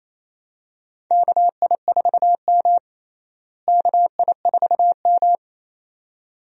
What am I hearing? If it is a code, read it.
Morse code, heard as a keyed tone, about 28 wpm, 700 Hz. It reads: KI4M KI4M
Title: USA Call Signs - 4 Characters - Repeated 2x - 28wpm